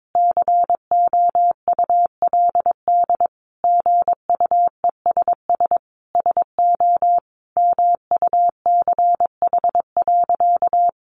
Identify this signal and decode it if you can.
Morse code, heard as a keyed tone, about 22 wpm, 700 Hz. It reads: COULD GUESS SO MUCH.
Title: Sentences from Top 1500 Words 22wpm